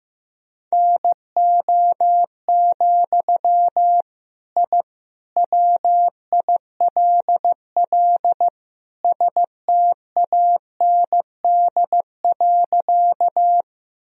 Morse code, heard as a keyed tone, about 15 wpm, 700 Hz. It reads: NO, I WILL STAND.
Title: Sentences from Top 1000 Words - Review Part 1 - 15wpm